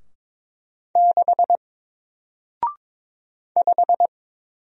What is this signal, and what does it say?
Morse code, heard as a keyed tone, about 22 wpm, 700 Hz. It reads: 6 5
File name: Single-Letter-Number-22wpm